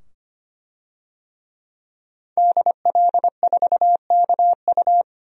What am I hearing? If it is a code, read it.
Morse code, heard as a keyed tone, about 25 wpm, 700 Hz. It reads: DL4KU
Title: DXCC-100 Call Signs - 5 Characters - 25wpm